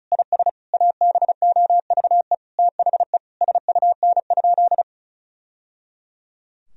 Morse code, heard as a keyed tone, about 36 wpm, 700 Hz. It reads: IS ABOVE THE SUN?
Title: Sentences from Top 1000 Words - Review Part 2 - 35wpm